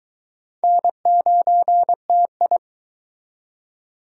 Morse code, heard as a keyed tone, about 23 wpm, 700 Hz. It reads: N9TI